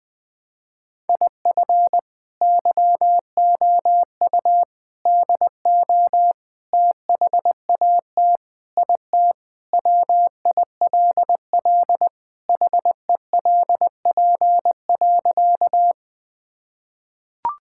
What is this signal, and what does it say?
Morse code, heard as a keyed tone, about 20 wpm, 700 Hz. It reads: IF YOU DO THAT IT WILL HELP.
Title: Sentences with 3-Word Phrases - 20wpm